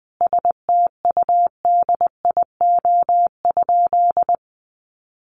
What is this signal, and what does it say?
Morse code, heard as a keyed tone, about 19 wpm, 700 Hz. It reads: STUDIO?
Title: Sentences from Top 1200 Words 20wpm